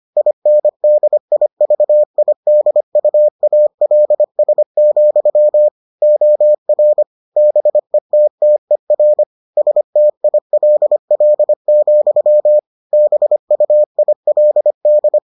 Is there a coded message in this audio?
Yes: Morse code from 0.2 to 15.2 s, heard as a keyed tone, about 25 wpm, 600 Hz. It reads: INDIVIDUALS, OR BETTER STILL, BUILD